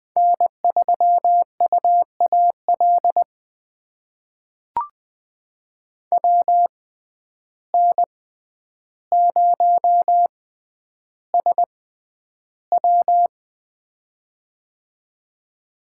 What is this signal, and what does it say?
Morse code, heard as a keyed tone, about 20 wpm, spaced as for 10 wpm, 700 Hz. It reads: N3UAL WN0SW
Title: USA Call Signs - 5 Characters - 6x Character Spacing - 20wpm